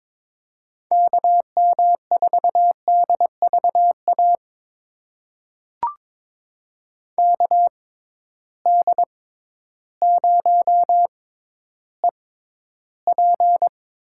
Morse code, heard as a keyed tone, about 21 wpm, spaced as for 12 wpm, 700 Hz. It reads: KM4DVA KD0EP
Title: USA Call Signs - 6 Characters - 6x Character Spacing - 22wpm